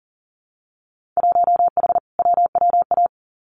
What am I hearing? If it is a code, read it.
Morse code, heard as a keyed tone, about 38 wpm, 700 Hz. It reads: 1H WWA
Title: Winter Field Day Exchanges 40wpm